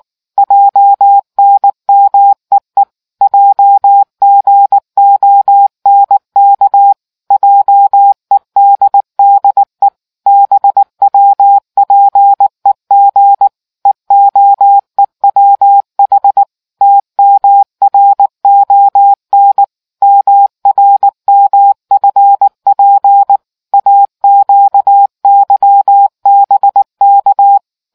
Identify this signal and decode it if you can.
Morse code, heard as a keyed tone, about 19 wpm, 800 Hz. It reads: JNMEE JGONK JEDDE BWPEG EOEWH TMRON MRMFP AQYBK